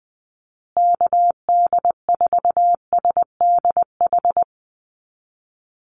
Morse code, heard as a keyed tone, about 20 wpm, 700 Hz. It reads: KD4SDH
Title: USA Call Signs Repeated 2x 20wpm